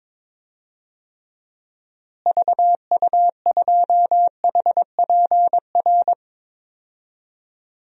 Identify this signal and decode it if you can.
Morse code, heard as a keyed tone, about 22 wpm, 700 Hz. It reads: VU2HPR
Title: DXCC-100 Call Signs - 6 Characters - Repeated 2x - 22wpm